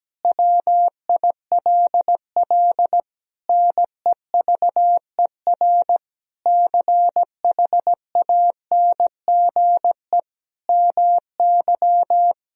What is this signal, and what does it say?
Morse code, heard as a keyed tone, about 17 wpm, 700 Hz. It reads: WILL NEVER CHANGE MY
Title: Sentences from Top 1000 Words - Review Part 1 - 17wpm